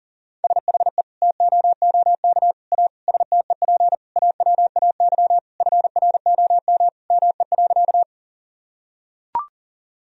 Morse code, heard as a keyed tone, about 40 wpm, 700 Hz. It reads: SHE TOOK A STEP AWAY FROM ME.
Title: Sentences from Top 1000 Words - Review Part 2 - 40wpm